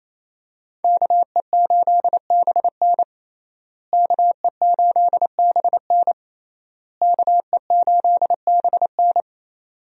Morse code, heard as a keyed tone, about 27 wpm, 700 Hz. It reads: KE8BN KE8BN KE8BN
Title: USA Call Signs - 5 Characters - Repeated 3x - 28wpm